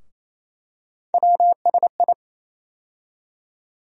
Morse code, heard as a keyed tone, about 28 wpm, 700 Hz. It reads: WSI
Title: USA Call Sign Postfixes - 3 Letters 28wpm